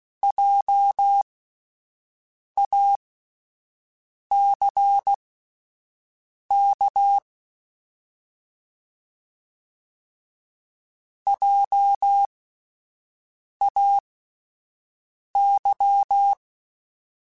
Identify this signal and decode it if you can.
Morse code, heard as a keyed tone, about 16 wpm, spaced as for 5 wpm, 800 Hz. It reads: JACK JAY